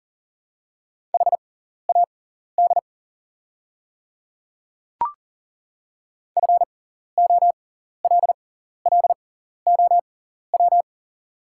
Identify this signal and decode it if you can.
Morse code, heard as a keyed tone, about 40 wpm, spaced as for 14 wpm, 690 Hz. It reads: HAD FOLLOW
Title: 6x Character Spacing - Top 200 Words - 40wpm